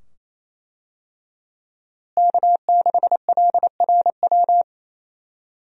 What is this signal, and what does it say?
Morse code, heard as a keyed tone, about 28 wpm, 700 Hz. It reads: K6LRW
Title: USA Call Signs - 5 Characters - Repeated 2x - 28wpm